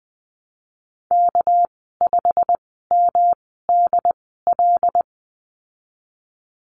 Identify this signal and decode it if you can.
Morse code, heard as a keyed tone, about 20 wpm, spaced as for 15 wpm, 700 Hz. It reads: K5MDL
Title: 207 Lesson 33g - Letter j - Call Signs 20wpm